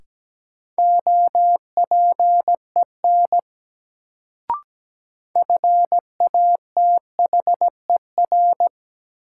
Morse code, heard as a keyed tone, about 17 wpm, 700 Hz. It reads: OPEN FATHER
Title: Sentences from Top 300 Words - New Word Drill - 17wpm